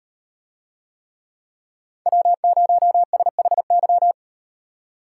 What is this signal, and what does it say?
Morse code, heard as a keyed tone, about 36 wpm, 700 Hz. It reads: W0SHY